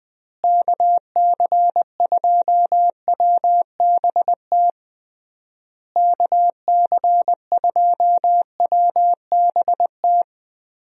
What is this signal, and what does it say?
Morse code, heard as a keyed tone, about 21 wpm, 700 Hz. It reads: KC2WBT KC2WBT
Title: USA Call Signs - 6 Characters - Repeated 3x - 20wpm